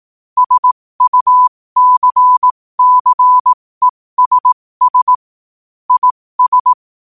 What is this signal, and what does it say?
Morse code, heard as a keyed tone, about 18 wpm, spaced as for 15 wpm, 1000 Hz. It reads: SUCCESS IS
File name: qotdfiles / 15wpm